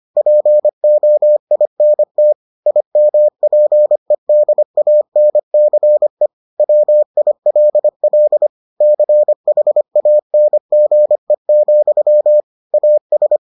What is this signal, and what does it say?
Morse code, heard as a keyed tone, about 25 wpm, 600 Hz. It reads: POINT IMPEDANCE WILL CHANGE, AS